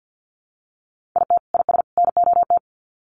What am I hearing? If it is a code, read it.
Morse code, heard as a keyed tone, about 50 wpm, 700 Hz. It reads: IT IS NOT
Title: Sets of 3 Words 50wpm